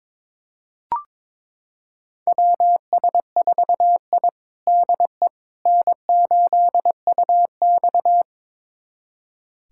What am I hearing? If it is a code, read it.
Morse code, heard as a keyed tone, about 22 wpm, 700 Hz. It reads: WS4I DE N8UX
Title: USA Callsign DE Callsign - 4 Characters - 22wpm